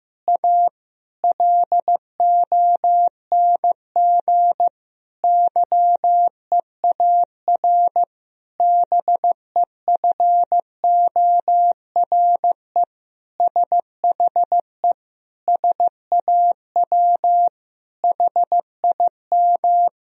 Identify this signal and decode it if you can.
Morse code, heard as a keyed tone, about 15 wpm, 700 Hz. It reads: A LONG YEAR BEFORE SHE SAW HIM
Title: Sentences from Top 1000 Words - Review Part 1 - 15wpm